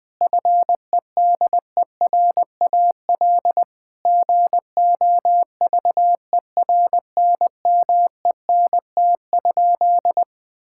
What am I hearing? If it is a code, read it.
Morse code, heard as a keyed tone, about 20 wpm, 700 Hz. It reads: FEDERAL GOVERNMENT?
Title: Sentences from Top 1000 Words - Review Part 5 - 20wpm